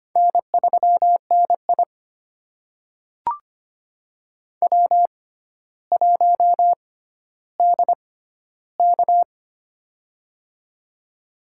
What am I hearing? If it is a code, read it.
Morse code, heard as a keyed tone, about 25 wpm, spaced as for 13 wpm, 700 Hz. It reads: N3NI W1DK
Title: USA Call Signs - 6x Character Spacing - 25wpm